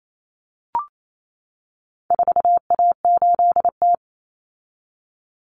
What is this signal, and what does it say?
Morse code, heard as a keyed tone, about 28 wpm, 700 Hz. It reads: 4A8T
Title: Common Cut Numbers - 4-Digit 28wpm